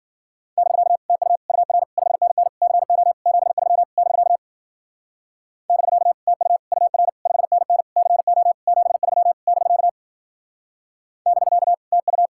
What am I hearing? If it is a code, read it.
Morse code, heard as a keyed tone, about 48 wpm, 700 Hz. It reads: <BK> TU UR 5NN CO 73 <BK> <BK> TU UR 5NN CO 73 <BK> <BK> TU
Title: Speed-Racing Common QSO Phrases Encore - 40wpm